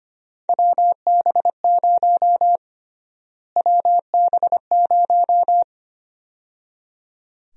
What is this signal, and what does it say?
Morse code, heard as a keyed tone, about 25 wpm, 700 Hz. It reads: WB0 WB0
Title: USA Call Sign Prefixes - Repeated 2x - 25wpm